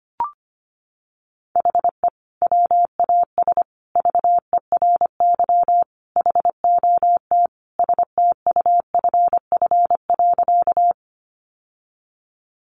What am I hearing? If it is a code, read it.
Morse code, heard as a keyed tone, about 25 wpm, 700 Hz. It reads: HE WAS VERY HOT STUFF.
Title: Sentences from Top 1400 Words 25wpm